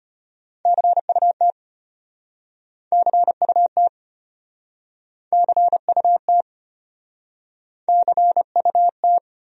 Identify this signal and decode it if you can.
Morse code, heard as a keyed tone, about 33 wpm, 700 Hz. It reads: CUT CUT CUT CUT